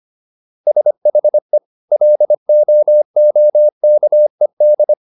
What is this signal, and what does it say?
Morse code, heard as a keyed tone, about 25 wpm, 600 Hz. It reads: SHE LOOKED
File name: SS123 25